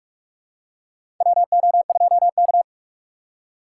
Morse code, heard as a keyed tone, about 46 wpm, 700 Hz. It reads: WO2K
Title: USA Call Signs - 4 Characters - 2x Character Spacing - 45wpm